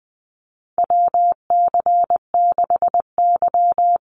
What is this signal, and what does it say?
Morse code, heard as a keyed tone, about 20 wpm, 700 Hz. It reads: WC6Y
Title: Call Signs Repeated 2x - 20wpm